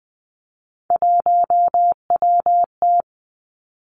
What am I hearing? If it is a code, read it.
Morse code, heard as a keyed tone, about 20 wpm, 700 Hz. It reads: 1WT
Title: Three Challenging Letter-Number Combinations 20wpm